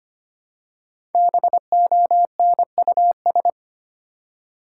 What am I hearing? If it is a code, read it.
Morse code, heard as a keyed tone, about 25 wpm, 700 Hz. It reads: BONUS
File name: 5 Letter Words - First 3 Letters Given - 25wpm